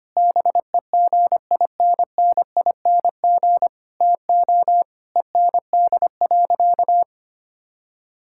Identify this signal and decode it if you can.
Morse code, heard as a keyed tone, about 25 wpm, 700 Hz. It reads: BEGINNING TO END.
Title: Sentences from Top 1000 Words - Review Part 5 - 25wpm